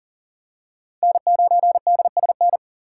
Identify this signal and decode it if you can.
Morse code, heard as a keyed tone, about 40 wpm, 700 Hz. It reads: N9DSN
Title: USA Call Signs - 5 Characters - 4x Character Spacing - 40wpm